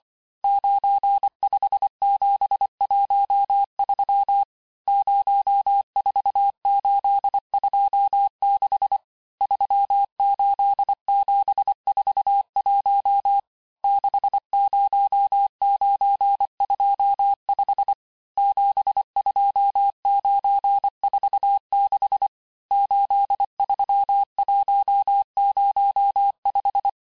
Morse code, heard as a keyed tone, about 25 wpm, 800 Hz. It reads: 95713 04826 38741 60925 72946 83105
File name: texte-chiffre-vitesse-23mpm